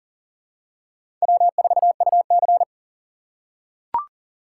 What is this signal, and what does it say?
Morse code, heard as a keyed tone, about 40 wpm, 700 Hz. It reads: W4UC
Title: USA Call Signs - 4 Characters - 3x Character Spacing - 40wpm